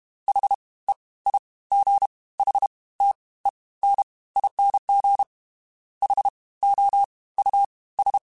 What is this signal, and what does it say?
Morse code, heard as a keyed tone, about 32 wpm, 800 Hz. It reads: H E I G H T E N ING H O U S
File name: Special INGs / Special INGs-mp3s / Fast ing @30